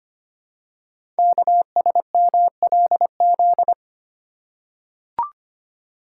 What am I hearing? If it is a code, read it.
Morse code, heard as a keyed tone, about 25 wpm, 700 Hz. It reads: KSMLZ
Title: Code Groups - 5 Letters - 25wpm